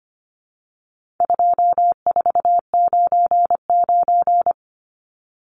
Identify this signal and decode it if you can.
Morse code, heard as a keyed tone, about 25 wpm, 700 Hz. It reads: 2499
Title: Numbers - 4-Digit 25wpm